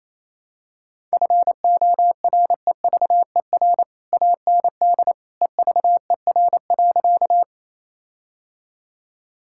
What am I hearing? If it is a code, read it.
Morse code, heard as a keyed tone, about 28 wpm, 700 Hz. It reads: FOREVER AND EVER.
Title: Sentences from Top 1600 Words 28wpm